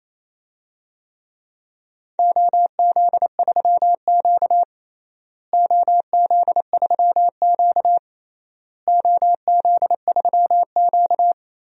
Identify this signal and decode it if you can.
Morse code, heard as a keyed tone, about 28 wpm, 700 Hz. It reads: OZ3Q OZ3Q OZ3Q
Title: DXCC-100 Call Signs - 4 Characters - Repeated 3x - 28wpm